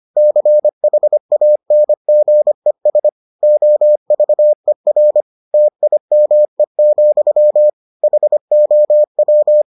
Morse code, heard as a keyed tone, about 25 wpm, 600 Hz. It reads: CHANGES OVER TIME, HOW